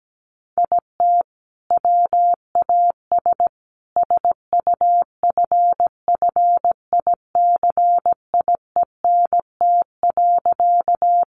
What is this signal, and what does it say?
Morse code, heard as a keyed tone, about 17 wpm, 700 Hz. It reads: IT WAS SUFFICIENT.
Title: Sentences from Top 1400 Words 17wpm